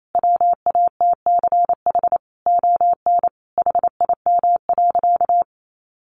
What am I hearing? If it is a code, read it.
Morse code, heard as a keyed tone, about 28 wpm, 700 Hz. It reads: WATCH ON HIM.
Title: Sentences from Top 1300 Words 28wpm